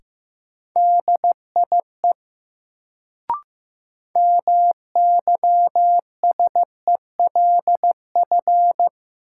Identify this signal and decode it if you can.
Morse code, heard as a keyed tone, about 15 wpm, 700 Hz. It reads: DIE MYSELF
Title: Sentences from Top 1300 Words - New Word Drill Alt - 15wpm